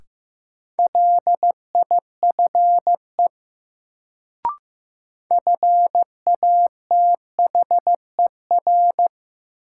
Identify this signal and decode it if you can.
Morse code, heard as a keyed tone, about 15 wpm, 700 Hz. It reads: LIFE FATHER
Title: Sentences from Top 300 Words - New Word Drill - 15wpm